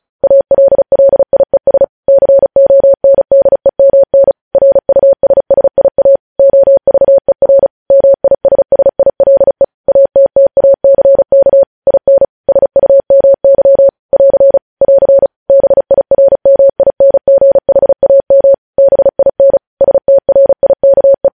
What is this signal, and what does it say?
Morse code, heard as a keyed tone, about 36 wpm, 550 Hz. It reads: ALLIES CONDEMN RUSSIA OVER MISSILE ATTACK IN SUMY <AR> <AR> BIRMINGHAM BIN STRIKE
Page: news headlines in morse code at 35 wpm